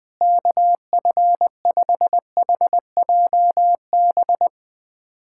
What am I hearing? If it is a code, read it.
Morse code, heard as a keyed tone, about 21 wpm, 700 Hz. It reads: KF5HJB
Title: USA Callsign DE Callsign - 6 Characters - 20wpm